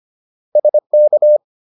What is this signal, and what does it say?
Morse code, heard as a keyed tone, about 25 wpm, 600 Hz. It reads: SK